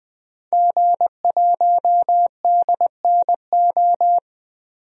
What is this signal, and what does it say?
Morse code, heard as a keyed tone, about 20 wpm, 700 Hz. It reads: G1DNO
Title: DXCC-100 Call Signs - 5 Characters - 20wpm